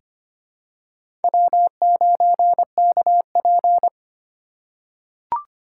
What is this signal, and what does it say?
Morse code, heard as a keyed tone, about 25 wpm, 700 Hz. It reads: W9KP
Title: USA Call Signs - 4 Characters - 8x Character Spacing - 25wpm